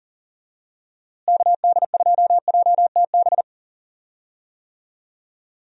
Morse code, heard as a keyed tone, about 37 wpm, 700 Hz. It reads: KD2JTB
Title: USA Call Signs - 6 Characters - Repeated 3x - 40wpm